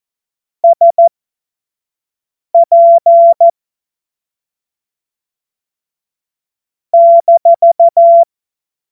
Morse code, heard as a keyed tone, about 14 wpm, spaced as for 5 wpm, 680 Hz. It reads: SP -